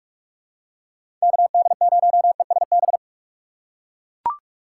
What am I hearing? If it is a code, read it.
Morse code, heard as a keyed tone, about 44 wpm, 700 Hz. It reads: KD0ESB
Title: USA Call Signs - 6 Characters - 3x Character Spacing - 45wpm